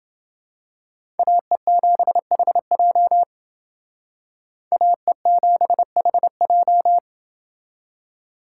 Morse code, heard as a keyed tone, about 28 wpm, 700 Hz. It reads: AE7HJ AE7HJ